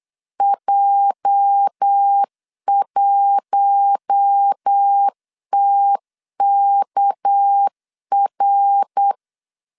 Morse code, heard as a keyed tone, about 8 wpm, 800 Hz. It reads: J1TKR
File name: LEZIONI / 8 YJ45 / 8 YJ 8wpm 3313